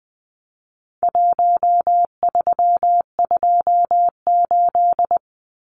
Morse code, heard as a keyed tone, about 20 wpm, 700 Hz. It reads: 1328
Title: Numbers - Contest Style - 4-Digit 20wpm